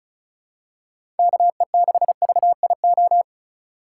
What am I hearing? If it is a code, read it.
Morse code, heard as a keyed tone, about 34 wpm, 700 Hz. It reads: KE6VIO
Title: USA Call Signs - 6 Characters - 2x Character Spacing - 35wpm